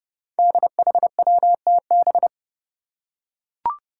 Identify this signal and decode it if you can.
Morse code, heard as a keyed tone, about 28 wpm, 700 Hz. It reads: DHWTB
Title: Five Challenging Letter-Number Combinations 30wpm